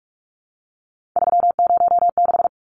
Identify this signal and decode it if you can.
Morse code, heard as a keyed tone, about 44 wpm, 700 Hz. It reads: 306
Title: Numbers - 3-Digit 45wpm